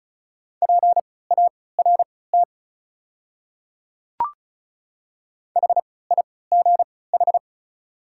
Morse code, heard as a keyed tone, about 36 wpm, spaced as for 20 wpm, 690 Hz. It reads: PART HIGH
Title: 3x Character Spacing - Top 300 Words - 35wpm